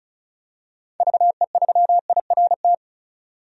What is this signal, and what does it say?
Morse code, heard as a keyed tone, about 34 wpm, 700 Hz. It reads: VE3IRT
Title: DXCC-100 Call Signs - 6 Characters - 35wpm